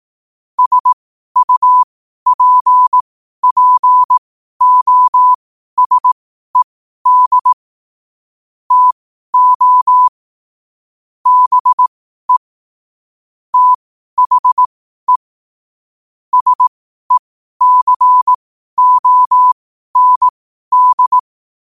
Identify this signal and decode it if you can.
Morse code, heard as a keyed tone, about 18 wpm, spaced as for 12 wpm, 1000 Hz. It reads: SUPPOSED TO BE THE SECOND
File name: qotdfiles / 12wpm